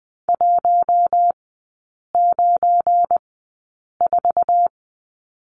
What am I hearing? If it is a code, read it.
Morse code, heard as a keyed tone, about 20 wpm, spaced as for 8 wpm, 700 Hz. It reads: 194
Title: 178 Lesson 29e - Slash - Three Characters 20wpm